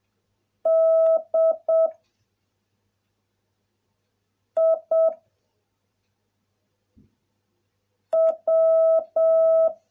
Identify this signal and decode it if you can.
Morse code, heard as a keyed tone, about 7 wpm, 660 Hz. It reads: D I W